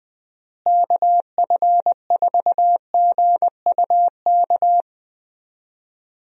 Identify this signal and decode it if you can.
Morse code, heard as a keyed tone, about 20 wpm, 700 Hz. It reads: KF4GUK
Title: USA Call Signs - 6 Characters - Repeated 2x - 20wpm